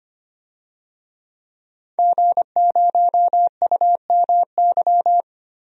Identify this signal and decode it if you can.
Morse code, heard as a keyed tone, about 25 wpm, 700 Hz. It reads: G0UMY
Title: DXCC-100 Call Signs - 5 Characters - 25wpm